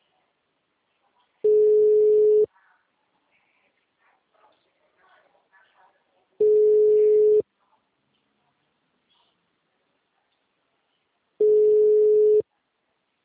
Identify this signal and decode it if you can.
Ringback tone (Europe, 425 Hz), with speech.
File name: out- -2006- - - .2